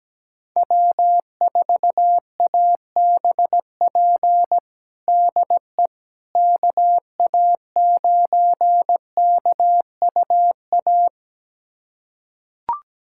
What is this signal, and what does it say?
Morse code, heard as a keyed tone, about 17 wpm, 700 Hz. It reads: W4ABP DE KA9KUA
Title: USA Callsign DE Callsign 17wpm